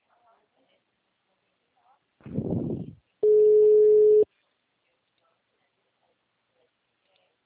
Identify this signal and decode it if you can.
Ringback tone (Europe, 425 Hz), with speech.